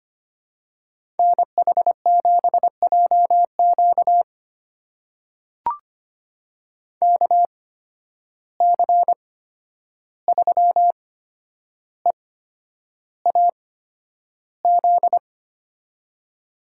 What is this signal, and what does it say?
Morse code, heard as a keyed tone, about 25 wpm, spaced as for 10 wpm, 700 Hz. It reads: NH7JQ KC3EAZ
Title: USA Call Signs - 8x Character Spacing - 25wpm